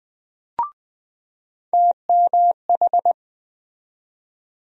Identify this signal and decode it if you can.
Morse code, heard as a keyed tone, about 20 wpm, 700 Hz. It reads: TMH
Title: USA Call Sign Postfixes - 3 Letters 20wpm